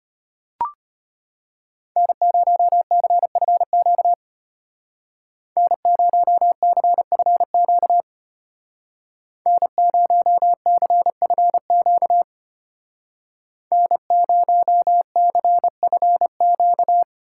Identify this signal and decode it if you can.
Morse code, heard as a keyed tone, about 31 wpm, 700 Hz. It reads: N0CFQ N0CFQ N0CFQ N0CFQ
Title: Speed Racing - USA 1x3 Call Signs 25wpm